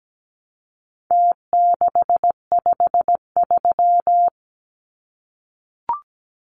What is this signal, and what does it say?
Morse code, heard as a keyed tone, about 17 wpm, 700 Hz. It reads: T653
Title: Numbers - Contest Style - 4-Digit 17wpm